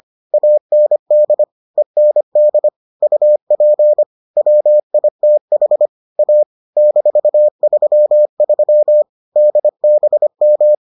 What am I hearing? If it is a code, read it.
Morse code, heard as a keyed tone, about 25 wpm, 600 Hz. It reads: AND END UP WITH A -33 DBM